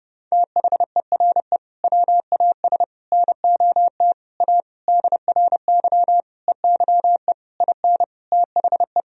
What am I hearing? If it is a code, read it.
Morse code, heard as a keyed tone, about 30 wpm, 700 Hz. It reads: THERE WAS NOT A DRY EYE IN THE
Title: Sentences from Top 1000 Words - Review Part 3 - 30wpm